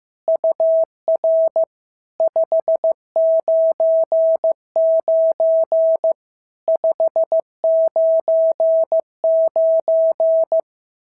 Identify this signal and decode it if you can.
Morse code, heard as a keyed tone, about 14 wpm, 650 Hz. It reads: UR 599 599
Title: SOTA QSO Practice 15wpm